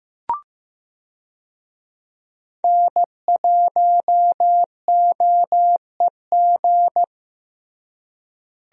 Morse code, heard as a keyed tone, about 15 wpm, 700 Hz. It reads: N1OEG
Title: USA Call Signs - 5 Characters - Repeated 2x - 15wpm